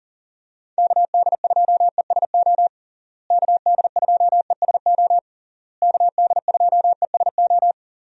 Morse code, heard as a keyed tone, about 40 wpm, 700 Hz. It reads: KD2ESO KD2ESO KD2ESO
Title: USA Call Signs - 6 Characters - Repeated 3x - 40wpm